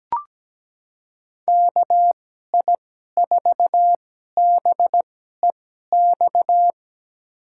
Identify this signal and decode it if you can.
Morse code, heard as a keyed tone, about 17 wpm, spaced as for 12 wpm, 700 Hz. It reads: KI4BEX
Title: USA Call Signs - 2x Character Spacing - 17wpm